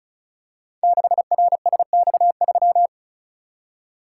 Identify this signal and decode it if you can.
Morse code, heard as a keyed tone, about 36 wpm, 700 Hz. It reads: 6RSX3
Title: Code Groups - 5 Letter:Numbers - 35wpm